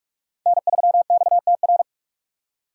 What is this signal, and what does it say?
Morse code, heard as a keyed tone, about 46 wpm, 700 Hz. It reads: N3XTR
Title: USA Call Signs - 4x Character Spacing - 45wpm